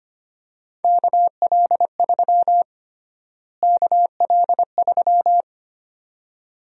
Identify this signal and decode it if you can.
Morse code, heard as a keyed tone, about 25 wpm, 700 Hz. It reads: KL3 KL3
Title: USA Call Sign Prefixes - Repeated 2x - 25wpm